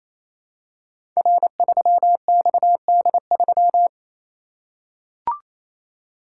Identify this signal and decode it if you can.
Morse code, heard as a keyed tone, about 27 wpm, 700 Hz. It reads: R3XD3
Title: Code Groups - 5 Letter:Numbers - 28wpm